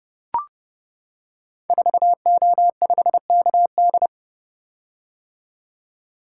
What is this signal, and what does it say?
Morse code, heard as a keyed tone, about 30 wpm, 700 Hz. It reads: 4O5KD
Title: Code Groups - 5 Letter:Numbers - 30wpm